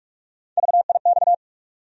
Morse code, heard as a keyed tone, about 46 wpm, 700 Hz. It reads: VIX
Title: USA Call Sign Postfixes - 3 Letters 45wpm